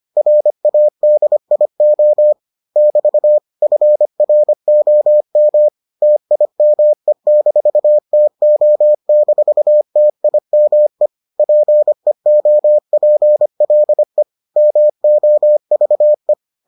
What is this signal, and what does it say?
Morse code, heard as a keyed tone, about 25 wpm, 600 Hz. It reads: RADIO <BT> FROM TIME-TO-TIME PEOPLE MOVE